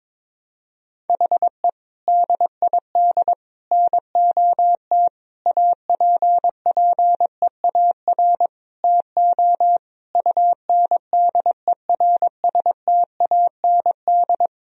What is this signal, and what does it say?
Morse code, heard as a keyed tone, about 22 wpm, 700 Hz. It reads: HE DID NOT APPEAR TO UNDERSTAND
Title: Sentences from Top 1000 Words - Review Part 2 - 22wpm